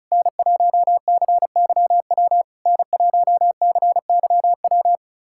Morse code, heard as a keyed tone, about 36 wpm, 690 Hz. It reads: N1CYW N1CYW
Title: SOTA QSO Practice 35wpm